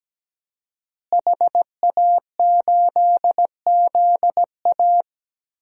Morse code, heard as a keyed tone, about 17 wpm, 700 Hz. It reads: HA8ZA
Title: DXCC-100 Call Signs - 5 Characters - Repeated 3x - 17wpm